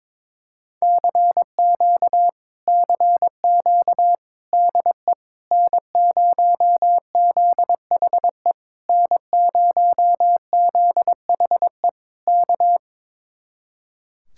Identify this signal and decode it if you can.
Morse code, heard as a keyed tone, about 21 wpm, 700 Hz. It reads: CQ CQ DE N0ZHE N0ZHE K
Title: Speed-Racing Common QSO Phrases Encore - 22wpm